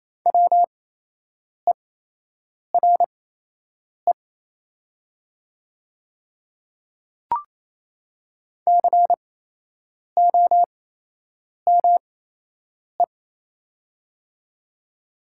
Morse code, heard as a keyed tone, about 28 wpm, spaced as for 8 wpm, 690 Hz. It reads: WERE COME
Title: 8x Character Spacing - Top 100 Words - 28wpm